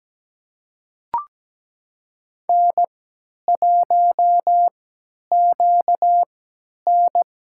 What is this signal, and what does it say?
Morse code, heard as a keyed tone, about 17 wpm, spaced as for 10 wpm, 700 Hz. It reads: N1QN
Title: USA Call Signs - 5 Characters - 3x Character Spacing - 17wpm